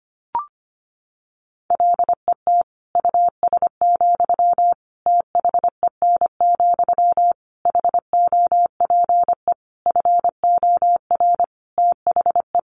Morse code, heard as a keyed tone, about 25 wpm, 700 Hz. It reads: LET US, THEN, HOPE FOR THE
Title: Sentences from Top 1000 Words 25wpm